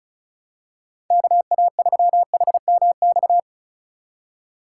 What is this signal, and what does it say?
Morse code, heard as a keyed tone, about 34 wpm, 700 Hz. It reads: KA3HMX
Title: USA Call Signs - 6 Characters - 4x Character Spacing - 35wpm